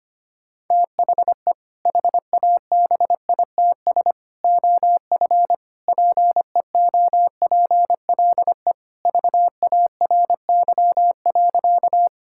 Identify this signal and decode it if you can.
Morse code, heard as a keyed tone, about 25 wpm, 700 Hz. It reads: THE HABITS OF PEOPLE VARY.
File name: practice / 25wpm / Sentences with 2-Word Phrases - 25wpm